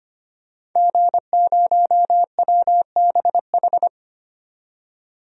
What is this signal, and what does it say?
Morse code, heard as a keyed tone, about 25 wpm, 700 Hz. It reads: G0WBH
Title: DXCC-100 Call Signs - 5 Characters - 25wpm